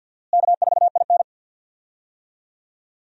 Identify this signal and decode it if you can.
Morse code, heard as a keyed tone, about 52 wpm, 700 Hz. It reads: K4IN
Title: USA Call Signs - 4 Characters - Repeated 3x - 50wpm